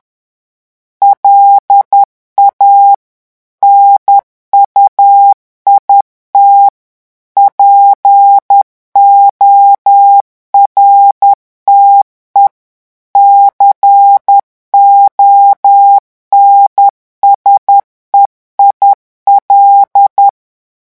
Morse code, heard as a keyed tone, about 10 wpm, 800 Hz. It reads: LA NUIT PORTE CONSEIL
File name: revision9 citations humoristiques vitesse 10 mots minute